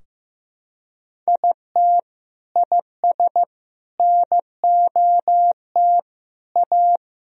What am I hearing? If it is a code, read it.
Morse code, heard as a keyed tone, about 15 wpm, 700 Hz. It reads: IT IS NOT A
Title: Sentences from Top 1000 Words - Review Part 4 - 15wpm